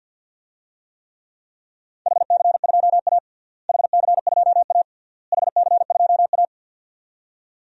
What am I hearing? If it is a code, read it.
Morse code, heard as a keyed tone, about 52 wpm, 700 Hz. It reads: HC2A HC2A HC2A
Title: DXCC-100 Call Signs - 4 Characters - Repeated 3x - 50wpm